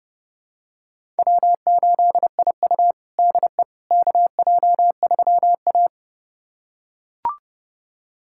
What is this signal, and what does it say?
Morse code, heard as a keyed tone, about 32 wpm, 700 Hz. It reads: W8IU DE KJ3A
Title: USA Callsign DE Callsign - 4 Characters - 30wpm